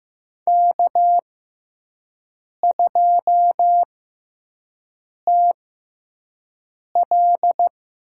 Morse code, heard as a keyed tone, about 15 wpm, spaced as for 5 wpm, 700 Hz. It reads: K2TL
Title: USA Call Signs - 5 Characters - 6x Character Spacing - 15wpm